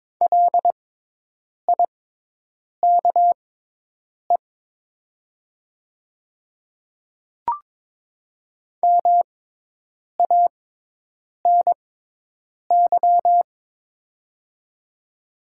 Morse code, heard as a keyed tone, about 22 wpm, spaced as for 8 wpm, 700 Hz. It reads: LIKE MANY
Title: 6x Character Spacing - Top 100 Words - 22wpm